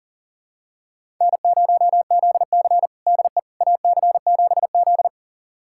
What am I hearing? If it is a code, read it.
Morse code, heard as a keyed tone, about 40 wpm, 700 Hz. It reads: N0ZC DE AC7Z
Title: USA Callsign DE Callsign - 4 Characters - 40wpm